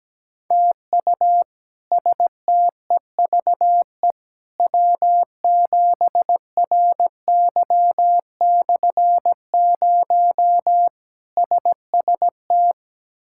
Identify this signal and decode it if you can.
Morse code, heard as a keyed tone, about 17 wpm, 700 Hz. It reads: TU STEVE W7RY/0 SST
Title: K1USN SST QSO Practice - 17wpm